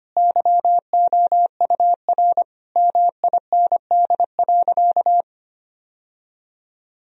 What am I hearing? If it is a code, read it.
Morse code, heard as a keyed tone, about 25 wpm, 700 Hz. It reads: YOUR MIND.
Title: Sentences from Top 1000 Words - Review Part 3 - 25wpm